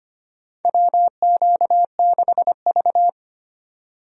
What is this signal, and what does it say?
Morse code, heard as a keyed tone, about 25 wpm, 700 Hz. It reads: WQ6V
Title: USA Call Signs - 4 Characters - 25wpm